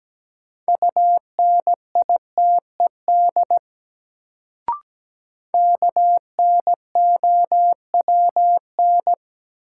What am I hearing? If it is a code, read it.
Morse code, heard as a keyed tone, about 17 wpm, 700 Hz. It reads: UNITED KNOWN
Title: Sentences from Top 600 Words - New Word Drill - 17wpm